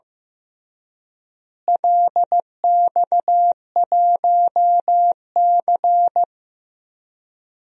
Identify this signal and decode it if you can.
Morse code, heard as a keyed tone, about 15 wpm, 700 Hz. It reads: LX1C
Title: DXCC-100 Call Signs - 4 Characters - Repeated 2x - 15wpm